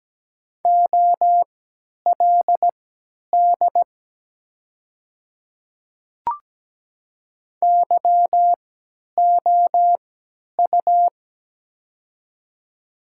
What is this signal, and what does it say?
Morse code, heard as a keyed tone, about 17 wpm, spaced as for 10 wpm, 690 Hz. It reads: OLD YOU
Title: 3x Character Spacing - Top 500 Words - 17wpm